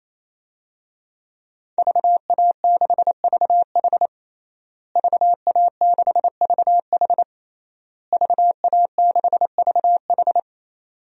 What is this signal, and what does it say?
Morse code, heard as a keyed tone, about 28 wpm, 700 Hz. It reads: VA6VH VA6VH VA6VH
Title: DXCC-100 Call Signs - 5 Characters - Repeated 3x - 28wpm